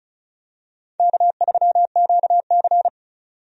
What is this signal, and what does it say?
Morse code, heard as a keyed tone, about 36 wpm, 700 Hz. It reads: K3QC
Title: USA Call Signs - 4 Characters - 35wpm